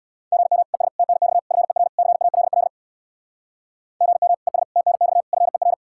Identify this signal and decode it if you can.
Morse code, heard as a keyed tone, about 64 wpm, 700 Hz. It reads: GM ES TNX FER CALL GM ES TNX FER